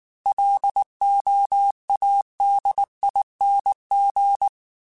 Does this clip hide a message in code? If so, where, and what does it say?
Yes: Morse code from 0.3 to 4.5 s, heard as a keyed tone, about 19 wpm, 800 Hz. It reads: LOADING